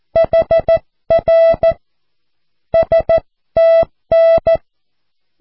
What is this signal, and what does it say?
Morse code, heard as a keyed tone, about 14 wpm, 660 Hz. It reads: HR STN
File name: cw / QSO